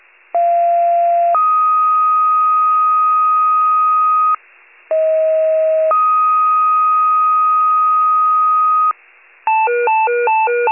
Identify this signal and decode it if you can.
Two-tone page, heard as a one-second tone, then a three-second tone, about 690 then 1250 Hz, about 640 then 1250 Hz. It is heard from 0.3 to 8.9 s.
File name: wcalert